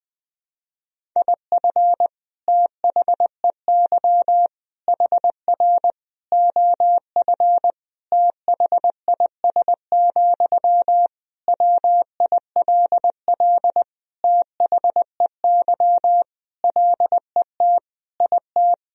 Morse code, heard as a keyed tone, about 20 wpm, 700 Hz. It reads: IF THEY HR OF THIS, WILL THEY LET IT
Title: Sentences from Top 400 Words and Abbreviations 20wpm